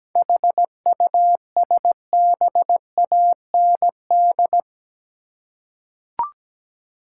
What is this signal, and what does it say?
Morse code, heard as a keyed tone, about 17 wpm, 700 Hz. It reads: HUSBAND
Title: Seven Letter Words - 17wpm